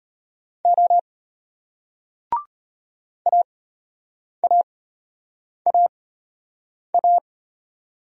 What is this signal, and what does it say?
Morse code, heard as a keyed tone, about 34 wpm, spaced as for 8 wpm, 700 Hz. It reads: O AAAA